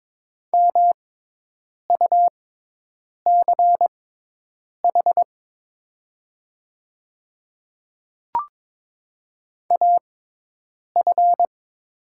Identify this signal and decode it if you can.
Morse code, heard as a keyed tone, about 22 wpm, spaced as for 8 wpm, 700 Hz. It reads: MUCH AF
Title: 6x Character Spacing - Top 300 Words - 22wpm